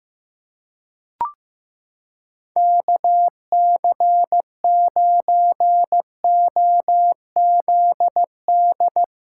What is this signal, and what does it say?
Morse code, heard as a keyed tone, about 15 wpm, 700 Hz. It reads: KC9OZD
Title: USA Callsign DE Callsign 15wpm